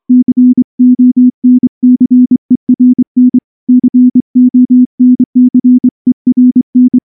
Morse code, heard as a keyed tone, about 24 wpm, 250 Hz. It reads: CONCERN CONCERN
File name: morse-training-25WPM